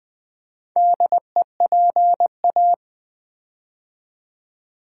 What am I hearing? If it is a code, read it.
Morse code, heard as a keyed tone, about 20 wpm, 700 Hz. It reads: DEPA
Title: Build A Word - 4 Letters At A Time - 10-Letter Words - 20wpm